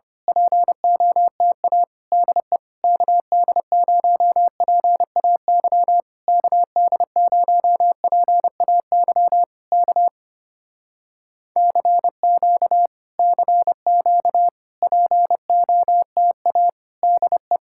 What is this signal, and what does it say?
Morse code, heard as a keyed tone, about 24 wpm, 700 Hz. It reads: POTA DE KD0PAY KD0PAY K CQ CQ POTA DE
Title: Speed-Racing Common QSO Phrases - 25wpm